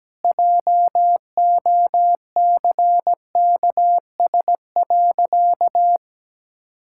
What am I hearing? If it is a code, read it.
Morse code, heard as a keyed tone, about 17 wpm, 700 Hz. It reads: JOCKS.
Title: Pangram Challenge v2 - 17wpm